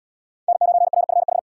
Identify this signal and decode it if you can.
Morse code, heard as a keyed tone, about 73 wpm, 700 Hz. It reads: N9RFH